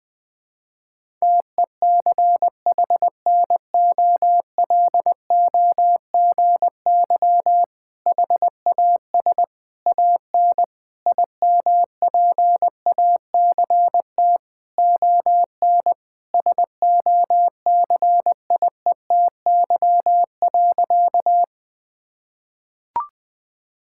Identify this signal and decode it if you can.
Morse code, heard as a keyed tone, about 21 wpm, 700 Hz. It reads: TECHNOLOGY HAS AN IMPACT ON SOCIETY.
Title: Sentences from Top 1000 Words - Review Part 6 - 20wpm